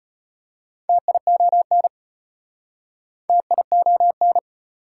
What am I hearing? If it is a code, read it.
Morse code, heard as a keyed tone, about 35 wpm, 690 Hz. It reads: TION TION